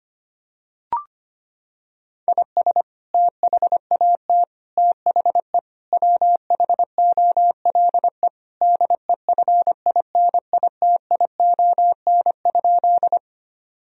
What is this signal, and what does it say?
Morse code, heard as a keyed tone, about 25 wpm, 700 Hz. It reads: IS THAT THE WHOLE DEFINITION?
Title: Sentences from Top 1000 Words - Review Part 4 - 25wpm